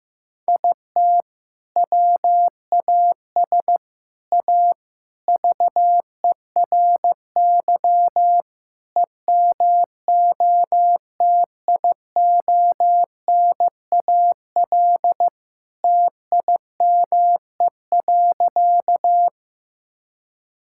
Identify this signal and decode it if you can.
Morse code, heard as a keyed tone, about 15 wpm, 700 Hz. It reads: IT WAS A VERY EMOTIONAL TIME.
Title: Sentences from Top 1000 Words - Review Part 5 - 15wpm